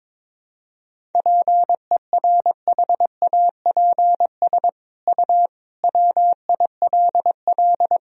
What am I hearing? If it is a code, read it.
Morse code, heard as a keyed tone, about 22 wpm, 700 Hz. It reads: PERHAPS U WILL
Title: Sentences from Top 500 Words and Abbreviations 22wpm